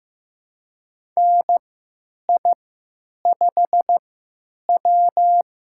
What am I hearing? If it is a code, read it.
Morse code, heard as a keyed tone, about 15 wpm, spaced as for 9 wpm, 700 Hz. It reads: NI5W
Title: USA Call Signs - 4 Characters - 3x Character Spacing - 15wpm